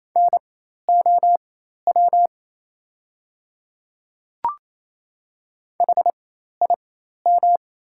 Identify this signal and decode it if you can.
Morse code, heard as a keyed tone, about 28 wpm, spaced as for 13 wpm, 700 Hz. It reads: NOW HIM
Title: 4x Character Spacing - Top 100 Words - 28wpm